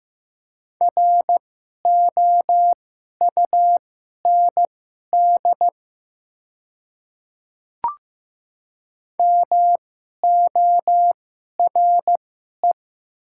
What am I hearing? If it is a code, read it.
Morse code, heard as a keyed tone, about 15 wpm, spaced as for 11 wpm, 700 Hz. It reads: ROUND MORE
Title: 2x Character Spacing - Top 200 Words - 15wpm